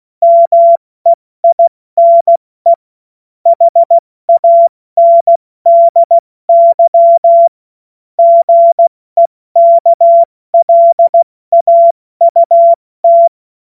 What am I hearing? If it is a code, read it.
Morse code, heard as a keyed tone, about 16 wpm, 680 Hz. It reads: MEINE HANDY GEKLAUT
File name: Morsekurs / Witze / Witz 22. Februar 2022 70 F80